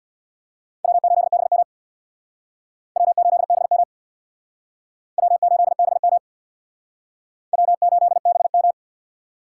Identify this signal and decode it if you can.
Morse code, heard as a keyed tone, about 63 wpm, 700 Hz. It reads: W8BM W8BM W8BM W8BM